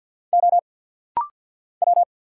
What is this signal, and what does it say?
Morse code, heard as a keyed tone, about 50 wpm, 690 Hz. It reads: O W
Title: Single Letters - Rapid-Fire 50wpm